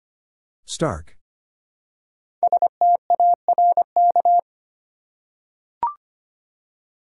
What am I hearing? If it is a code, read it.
Morse code, heard as a keyed tone, about 25 wpm, 700 Hz. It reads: STARK
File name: 5 Letter Words - First 3 Letters Given - 25wpm